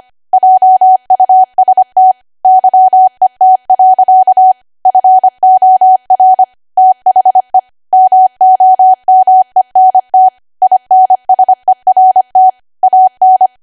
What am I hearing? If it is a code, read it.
Morse code, heard as a keyed tone, about 25 wpm, 750 Hz. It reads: JUST YET. FOR THE MOMENT INSERT AN
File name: Archive / 25 WPM / 25WPM